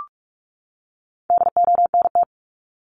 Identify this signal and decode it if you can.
Morse code, heard as a keyed tone, about 44 wpm, 700 Hz. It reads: DONT
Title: Top 500 Words 45wpm